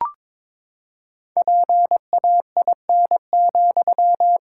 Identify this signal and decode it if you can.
Morse code, heard as a keyed tone, about 22 wpm, 700 Hz. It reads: PAIN,
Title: Sentences from Top 1600 Words 22wpm